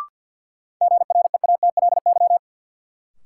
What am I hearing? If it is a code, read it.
Morse code, heard as a keyed tone, about 52 wpm, 700 Hz. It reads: GREATLY
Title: Sentences from Top 1400 Words - New Word Drill - 50wpm